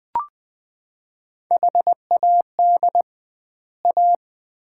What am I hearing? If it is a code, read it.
Morse code, heard as a keyed tone, about 20 wpm, 700 Hz. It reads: HAD A
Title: 2x Word-Spacing - Sets of 2 Words - 20wpm